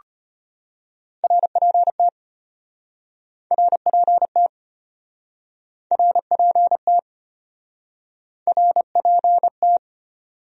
Morse code, heard as a keyed tone, about 33 wpm, 700 Hz. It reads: RPT RPT RPT RPT